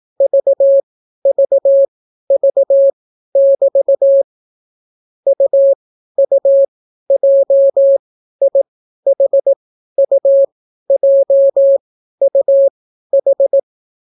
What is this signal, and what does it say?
Morse code, heard as a keyed tone, about 18 wpm, spaced as for 12 wpm, 550 Hz. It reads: VVV<BT> UUJIHUJUH